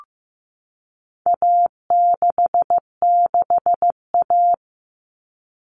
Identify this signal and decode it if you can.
Morse code, heard as a keyed tone, about 15 wpm, 700 Hz. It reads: A66A
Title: All Cut Numbers - 4-Digit 15wpm